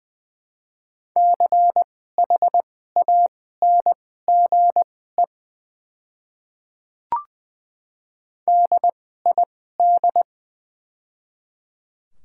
Morse code, heard as a keyed tone, about 20 wpm, spaced as for 14 wpm, 700 Hz. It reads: CHANGE DID
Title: 2x Character Spacing - Top 300 Words - 20wpm